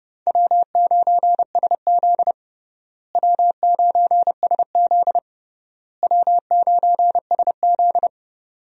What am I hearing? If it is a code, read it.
Morse code, heard as a keyed tone, about 30 wpm, 700 Hz. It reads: W9SZ W9SZ W9SZ
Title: USA Call Signs - 4 Characters - Repeated 3x - 30wpm